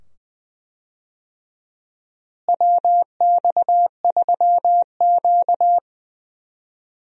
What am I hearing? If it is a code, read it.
Morse code, heard as a keyed tone, about 20 wpm, 700 Hz. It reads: WX3Q
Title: USA Call Signs - 4 Characters - Repeated 2x - 20wpm